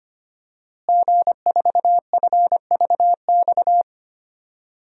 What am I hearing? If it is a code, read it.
Morse code, heard as a keyed tone, about 25 wpm, 700 Hz. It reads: G4FVX
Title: DXCC-100 Call Signs - 5 Characters - Repeated 2x - 25wpm